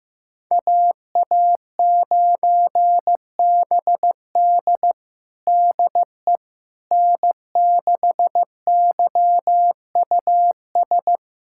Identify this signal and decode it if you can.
Morse code, heard as a keyed tone, about 15 wpm, 700 Hz. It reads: AA9BD DE N6YUS
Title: USA Callsign DE Callsign 15wpm